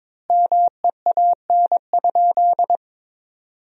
Morse code, heard as a keyed tone, about 22 wpm, 700 Hz. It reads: MEAN?
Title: 4x Word-Spacing - Sentences from Top 500 Words - 22wpm